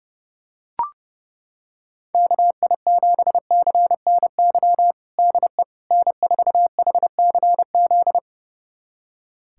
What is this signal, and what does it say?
Morse code, heard as a keyed tone, about 30 wpm, 690 Hz. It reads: KI7CNY DE N4HCZ
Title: USA Callsign DE Callsign 30wpm